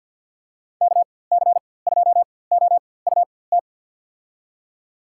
Morse code, heard as a keyed tone, about 52 wpm, spaced as for 23 wpm, 700 Hz. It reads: KC2OUT
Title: USA Call Signs - 6 Characters - 4x Character Spacing - 50wpm